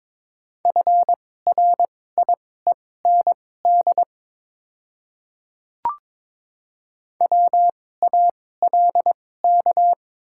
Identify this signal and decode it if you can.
Morse code, heard as a keyed tone, about 22 wpm, spaced as for 16 wpm, 700 Hz. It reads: FRIEND WALK
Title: 2x Character Spacing - Top 300 Words - 22wpm